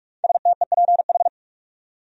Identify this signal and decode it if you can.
Morse code, heard as a keyed tone, about 44 wpm, 700 Hz. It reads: STEPH
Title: USA Common Names - 45wpm